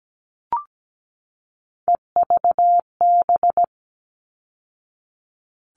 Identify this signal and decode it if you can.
Morse code, heard as a keyed tone, about 17 wpm, 700 Hz. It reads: EVB
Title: All Cut Numbers - 3-Digit 17wpm